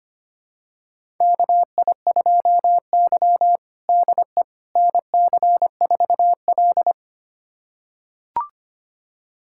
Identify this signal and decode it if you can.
Morse code, heard as a keyed tone, about 25 wpm, 700 Hz. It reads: KI2Y DE NC4L
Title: USA Callsign DE Callsign - 4 Characters - 25wpm